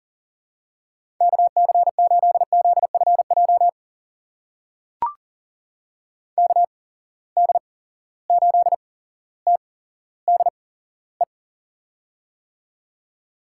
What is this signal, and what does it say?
Morse code, heard as a keyed tone, about 40 wpm, spaced as for 17 wpm, 700 Hz. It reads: KC8ZFJ KD8TDE
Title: USA Call Signs - 6 Characters - 8x Character Spacing - 40wpm